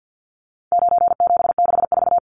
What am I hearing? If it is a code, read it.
Morse code, heard as a keyed tone, about 50 wpm, 700 Hz. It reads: 9764
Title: Numbers - 4-Digit 50wpm